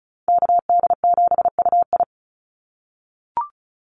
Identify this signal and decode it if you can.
Morse code, heard as a keyed tone, about 34 wpm, 700 Hz. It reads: KD7UI
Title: Call Signs - 35wpm